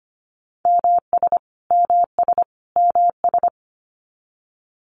Morse code, heard as a keyed tone, about 25 wpm, 700 Hz. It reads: MS MS MS
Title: US States Repeated 3x 25wpm